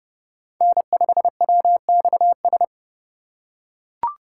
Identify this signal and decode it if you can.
Morse code, heard as a keyed tone, about 30 wpm, 700 Hz. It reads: N5WXS
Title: USA Call Signs - 5 Characters - 4x Character Spacing - 30wpm